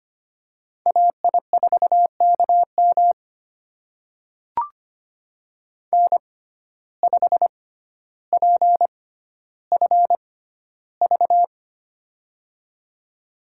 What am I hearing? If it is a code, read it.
Morse code, heard as a keyed tone, about 25 wpm, spaced as for 13 wpm, 700 Hz. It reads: AI4KM N5PFV
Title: USA Call Signs - 5 Characters - 6x Character Spacing - 25wpm